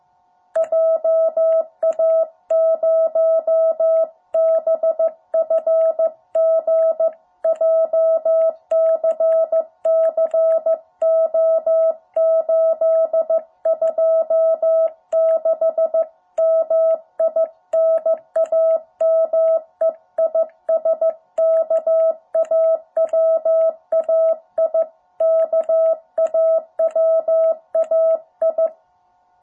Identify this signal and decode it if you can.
Morse code, heard as a keyed tone, about 14 wpm, 650 Hz. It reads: JA0BFG JCC O826 MI NAME IS KAWAI KAWAI
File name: jh8cbh / 2021cw